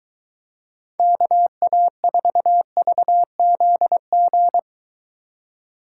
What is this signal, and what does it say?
Morse code, heard as a keyed tone, about 23 wpm, 700 Hz. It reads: KA4VZG